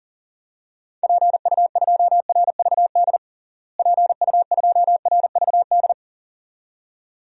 Morse code, heard as a keyed tone, about 40 wpm, 700 Hz. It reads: PU2RVD PU2RVD
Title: DXCC-100 Call Signs - 6 Characters - Repeated 2x - 40wpm